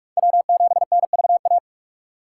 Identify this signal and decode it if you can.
Morse code, heard as a keyed tone, about 46 wpm, 700 Hz. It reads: W7NVA
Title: USA Call Signs - 5 Characters - 2x Character Spacing - 45wpm